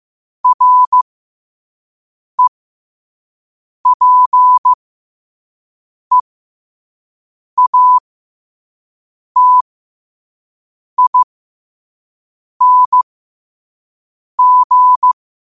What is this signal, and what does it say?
Morse code, heard as a keyed tone, about 15 wpm, spaced as for 5 wpm, 1000 Hz. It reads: REPEATING
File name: qotdfiles / 5wpm